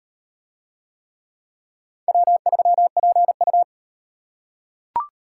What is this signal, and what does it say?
Morse code, heard as a keyed tone, about 38 wpm, 700 Hz. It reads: W3PU